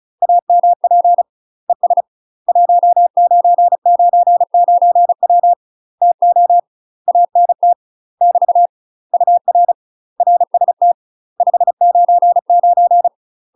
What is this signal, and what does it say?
Morse code, heard as a keyed tone, about 34 wpm, 700 Hz. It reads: AMP ES 1999W TO ANT <BT> UR RST 599
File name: UK MASTER QSOs TRAINING / 35WPM / UK MASTER QSOs 0000